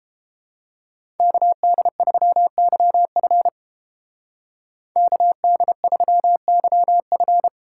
Morse code, heard as a keyed tone, about 29 wpm, 700 Hz. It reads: KD3YF KD3YF